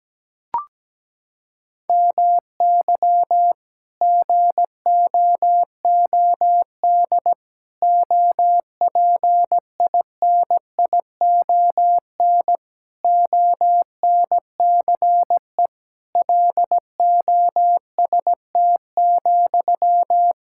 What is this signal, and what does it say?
Morse code, heard as a keyed tone, about 17 wpm, 700 Hz. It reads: MY GOOD OPINION ONCE LOST,
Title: Sentences from Top 1600 Words 17wpm